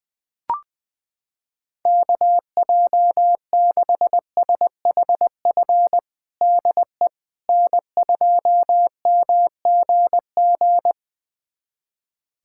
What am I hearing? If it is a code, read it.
Morse code, heard as a keyed tone, about 21 wpm, 700 Hz. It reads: KJ6SHF DE N2MGG
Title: USA Callsign DE Callsign 20wpm